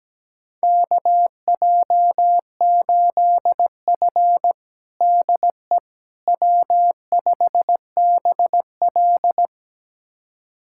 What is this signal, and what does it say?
Morse code, heard as a keyed tone, about 17 wpm, 700 Hz. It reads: KJ8F DE W5BL
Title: USA Callsign DE Callsign - 4 Characters - 17wpm